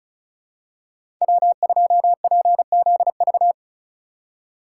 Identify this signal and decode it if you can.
Morse code, heard as a keyed tone, about 34 wpm, 700 Hz. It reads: W2PZV
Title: USA Call Signs - 5 Characters - 4x Character Spacing - 35wpm